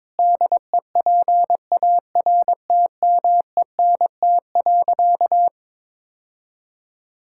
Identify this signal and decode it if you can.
Morse code, heard as a keyed tone, about 22 wpm, 700 Hz. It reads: DEPARTMENT.
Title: Sentences from Top 1000 Words - Review Part 5 - 22wpm